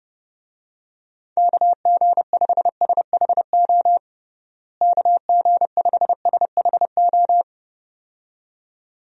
Morse code, heard as a keyed tone, about 30 wpm, 700 Hz. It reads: KG5SHO KG5SHO
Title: USA Call Signs - 6 Characters - Repeated 2x - 30wpm